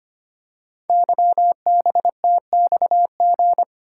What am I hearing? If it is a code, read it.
Morse code, heard as a keyed tone, about 25 wpm, 700 Hz. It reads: YBTXG
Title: Code Groups - 5 Letters - 25wpm